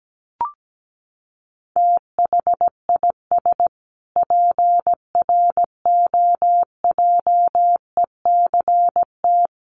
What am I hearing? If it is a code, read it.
Morse code, heard as a keyed tone, about 17 wpm, 700 Hz. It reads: THIS PROJECT
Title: Sentences from Top 1400 Words 17wpm